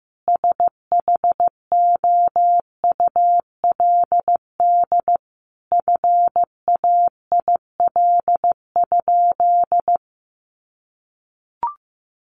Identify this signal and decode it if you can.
Morse code, heard as a keyed tone, about 15 wpm, 700 Hz. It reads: SHOULD FAIL?
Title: Sentences from Top 1100 Words 15wpm